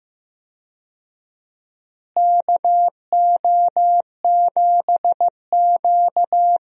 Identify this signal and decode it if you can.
Morse code, heard as a keyed tone, about 15 wpm, 700 Hz. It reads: KO7Q
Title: USA Call Signs - 4 Characters - Repeated 2x - 15wpm